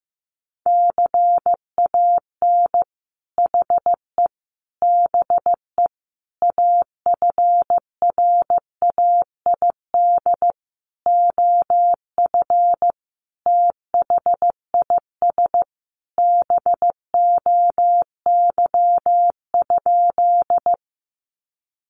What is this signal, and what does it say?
Morse code, heard as a keyed tone, about 15 wpm, 700 Hz. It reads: CAN HE BE AFRAID OF THIS BOY?
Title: Sentences from Top 1500 Words 15wpm